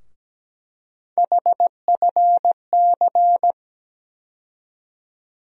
Morse code, heard as a keyed tone, about 17 wpm, 700 Hz. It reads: HFC
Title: USA Call Sign Postfixes - 3 Letters 17wpm